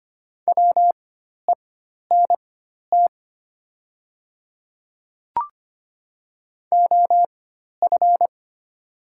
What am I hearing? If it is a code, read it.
Morse code, heard as a keyed tone, about 25 wpm, spaced as for 12 wpm, 700 Hz. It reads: WENT OF
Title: 4x Character Spacing - Top 200 Words - 25wpm